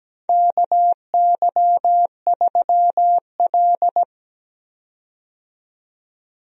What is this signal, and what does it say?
Morse code, heard as a keyed tone, about 17 wpm, 700 Hz. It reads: KY3L
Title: USA Call Signs - 4 Characters - 17wpm